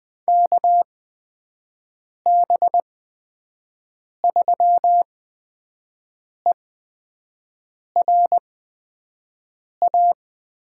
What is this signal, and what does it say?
Morse code, heard as a keyed tone, about 20 wpm, spaced as for 5 wpm, 700 Hz. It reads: KB3ERA
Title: USA Call Signs - 6 Characters - 8x Character Spacing - 20wpm